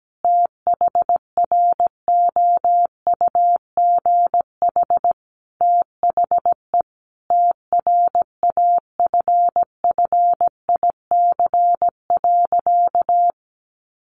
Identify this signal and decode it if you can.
Morse code, heard as a keyed tone, about 17 wpm, 700 Hz. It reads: THROUGH THE TRAFFIC.
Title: Sentences from Top 1300 Words 17wpm